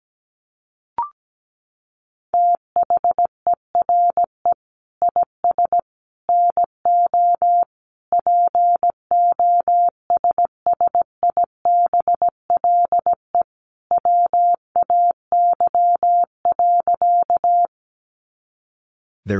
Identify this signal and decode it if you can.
Morse code, heard as a keyed tone, about 17 wpm, 700 Hz. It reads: THERE IS NO POSSIBLE WAY.
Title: Sentences from Top 500 Words 17wpm